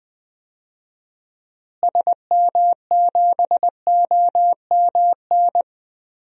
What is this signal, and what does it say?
Morse code, heard as a keyed tone, about 20 wpm, 700 Hz. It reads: SM7OMN
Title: DXCC-100 Call Signs - 6 Characters - Repeated 3x - 20wpm